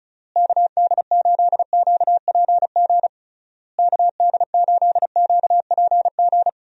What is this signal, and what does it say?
Morse code, heard as a keyed tone, about 36 wpm, 700 Hz. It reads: KD8QPG KD8QPG
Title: USA Call Signs - 6 Characters - Repeated 3x - 35wpm